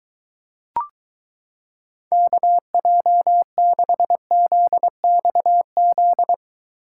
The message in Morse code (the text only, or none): KJ6ZXZ